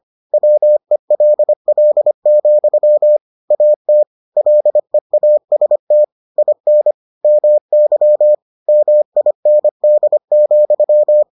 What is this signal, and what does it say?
Morse code, heard as a keyed tone, about 25 wpm, 600 Hz. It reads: WELL, AT LEAST IN MY MIND,